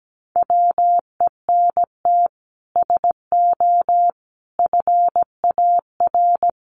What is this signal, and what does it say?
Morse code, heard as a keyed tone, about 17 wpm, 700 Hz. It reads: WENT SO FAR
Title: Sets of 5 Words Encore 17wpm